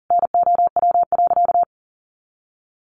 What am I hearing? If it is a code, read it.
Morse code, heard as a keyed tone, about 40 wpm, 700 Hz. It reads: NOW.
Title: Sentences from Top 200 Words 40wpm